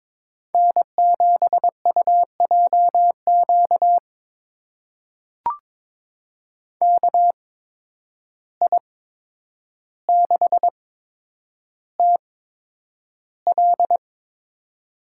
Morse code, heard as a keyed tone, about 22 wpm, spaced as for 9 wpm, 700 Hz. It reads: N7UJQ KI6TL
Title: USA Call Signs - 5 Characters - 8x Character Spacing - 22wpm